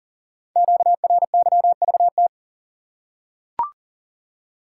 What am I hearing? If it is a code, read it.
Morse code, heard as a keyed tone, about 38 wpm, 700 Hz. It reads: QRYVT
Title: Code Groups - 5 Letter:Numbers - 40wpm